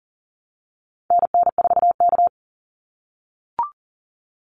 Morse code, heard as a keyed tone, about 40 wpm, 700 Hz. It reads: NN4K
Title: Call Signs - 40wpm